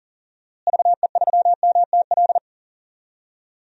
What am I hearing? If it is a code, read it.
Morse code, heard as a keyed tone, about 40 wpm, 700 Hz. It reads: VE3MTL
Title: DXCC-100 Call Signs - 6 Characters - Repeated 2x - 40wpm